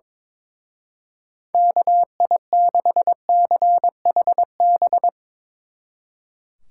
Morse code, heard as a keyed tone, about 24 wpm, 700 Hz. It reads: KI6CHB
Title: Speed Racing - USA 2x3 Call Signs 22wpm